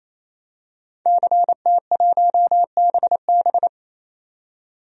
Morse code, heard as a keyed tone, about 28 wpm, 690 Hz. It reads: CT1BB
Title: DXCC-100 Call Signs - 5 Characters - 28wpm